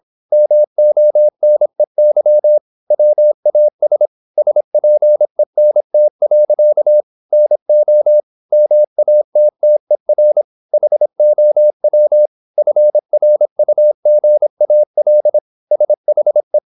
Morse code, heard as a keyed tone, about 26 wpm, 600 Hz. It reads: MONEY WAS SPENT. NO MATTER HOW FRUGAL SHE
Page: Never 26wpm | CW med Gnister